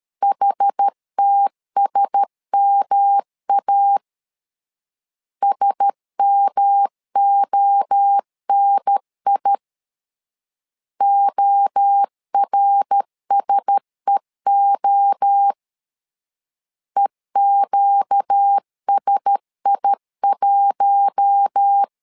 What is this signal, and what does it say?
Morse code, heard as a keyed tone, about 13 wpm, 800 Hz. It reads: HTSMA SMONI ORSEO EQSI1
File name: LEZIONI / 4 Q1S 12wpm 1802